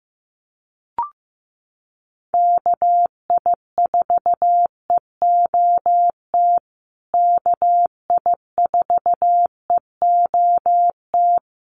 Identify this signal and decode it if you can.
Morse code, heard as a keyed tone, about 15 wpm, 700 Hz. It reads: KI4EOT KI4EOT
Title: USA Call Signs Repeated 3x 15wpm